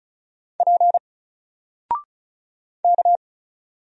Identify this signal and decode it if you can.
Morse code, heard as a keyed tone, about 36 wpm, 700 Hz. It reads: P K
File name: Single-Letter-Number-35wpm